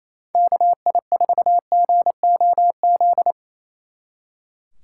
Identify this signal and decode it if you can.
Morse code, heard as a keyed tone, about 36 wpm, 700 Hz. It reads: KI4GOZ